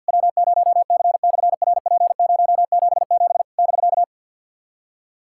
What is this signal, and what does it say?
Morse code, heard as a keyed tone, about 53 wpm, 700 Hz. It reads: W0C/RP077 <BK>
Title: SOTA QSO Practice 50wpm